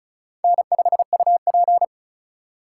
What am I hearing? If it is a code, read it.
Morse code, heard as a keyed tone, about 36 wpm, 700 Hz. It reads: N5UP
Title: USA Call Signs - 4 Characters - 2x Character Spacing - 35wpm